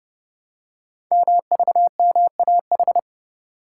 Morse code, heard as a keyed tone, about 30 wpm, 700 Hz. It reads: MVMAH
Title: Code Groups - 5 Letters - 30wpm